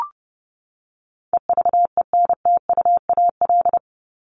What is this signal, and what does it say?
Morse code, heard as a keyed tone, about 30 wpm, 700 Hz. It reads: EVENTUAL
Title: LY Suffix - Top 100 Words with Base Words 30wpm